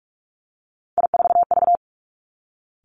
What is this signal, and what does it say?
Morse code, heard as a keyed tone, about 46 wpm, 700 Hz. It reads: I4V
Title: Three Challenging Letter-Number Combinations 45wpm